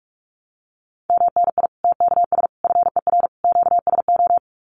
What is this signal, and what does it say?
Morse code, heard as a keyed tone, about 44 wpm, 700 Hz. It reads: MNI TKS FER QSO
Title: Common QSO Phrases 45wpm